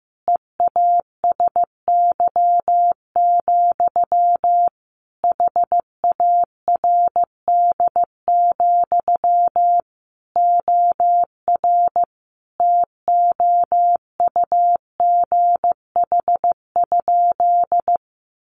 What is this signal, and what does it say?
Morse code, heard as a keyed tone, about 14 wpm, 700 Hz. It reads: EASY, HARD, OR TOUGH?
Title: Sentences from Top 1200 Words 15wpm